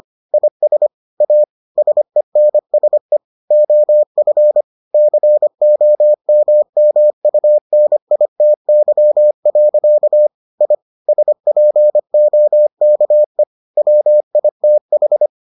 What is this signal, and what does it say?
Morse code, heard as a keyed tone, about 25 wpm, 600 Hz. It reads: IS A SENSE OF COMMUNITY. I SPOKE WITH